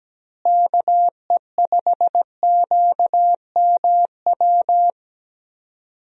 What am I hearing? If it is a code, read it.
Morse code, heard as a keyed tone, about 17 wpm, 700 Hz. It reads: KE5QMW
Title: USA Call Signs - 6 Characters - Repeated 3x - 17wpm